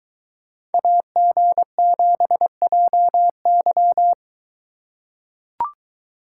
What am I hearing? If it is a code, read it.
Morse code, heard as a keyed tone, about 22 wpm, 700 Hz. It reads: AG7JY